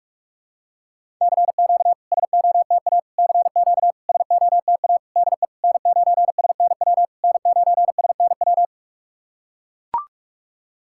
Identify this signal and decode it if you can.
Morse code, heard as a keyed tone, about 44 wpm, 690 Hz. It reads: CQ SOTA CQ SOTA DE N9SNW N9SNW
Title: SOTA QSO Practice 45wpm